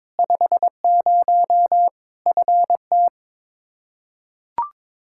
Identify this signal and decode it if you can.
Morse code, heard as a keyed tone, about 22 wpm, 700 Hz. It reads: 50 FT
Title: K1USN SST QSO Practice 22wpm